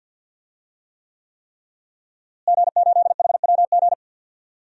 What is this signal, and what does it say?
Morse code, heard as a keyed tone, about 48 wpm, 700 Hz. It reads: G8HWG
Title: DXCC-100 Call Signs - 5 Characters - Repeated 3x - 50wpm